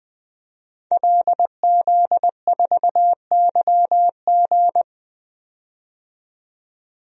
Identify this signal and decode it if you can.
Morse code, heard as a keyed tone, about 20 wpm, 700 Hz. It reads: LZ4YG
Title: DXCC-100 Call Signs - 5 Characters - 20wpm